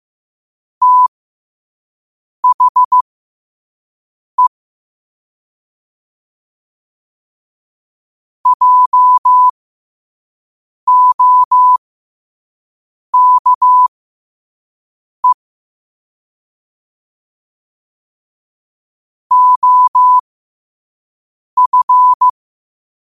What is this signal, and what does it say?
Morse code, heard as a keyed tone, about 15 wpm, spaced as for 5 wpm, 1000 Hz. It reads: THE JOKE OF
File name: qotdfiles / 5wpm